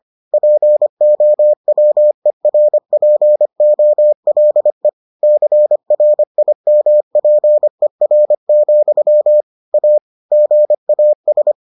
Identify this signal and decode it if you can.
Morse code, heard as a keyed tone, about 25 wpm, 600 Hz. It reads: POWERPOLE CRIMPER, A GAS